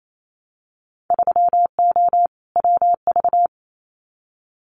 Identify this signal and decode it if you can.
Morse code, heard as a keyed tone, about 28 wpm, 700 Hz. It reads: 3O WV
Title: Winter Field Day Exchanges 28wpm